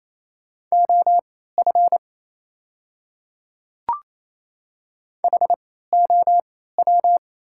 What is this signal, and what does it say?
Morse code, heard as a keyed tone, about 28 wpm, spaced as for 16 wpm, 700 Hz. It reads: OF HOW
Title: 3x Character Spacing - Top 100 Words - 28wpm